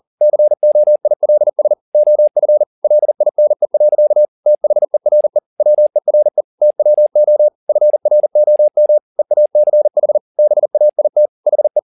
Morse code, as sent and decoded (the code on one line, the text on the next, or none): -.-. --- .. .-.. ... / --- ..-. / .-.. .. -. .-.-.- / - .... .-. / .-- . .-. / - .-- --- / ..-. .-. --- -- / . .- -.-. .... / -... .- .. - / .... .
COILS OF LINE. THERE WERE TWO FROM EACH BAIT HE